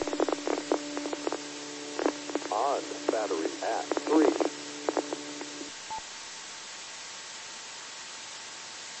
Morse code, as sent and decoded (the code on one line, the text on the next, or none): none